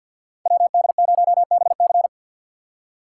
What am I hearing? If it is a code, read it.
Morse code, heard as a keyed tone, about 62 wpm, 700 Hz. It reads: WD0BC
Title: Speed Racing - USA 2x2 Call Signs 50wpm